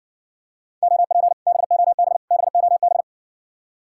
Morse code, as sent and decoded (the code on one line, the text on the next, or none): --- .--. / -... --- -... / -... --- -...
OP BOB BOB